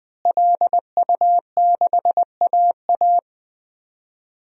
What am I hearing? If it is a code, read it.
Morse code, heard as a keyed tone, about 20 wpm, 700 Hz. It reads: LU6AA
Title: DXCC-100 Call Signs - 5 Characters - Repeated 3x - 20wpm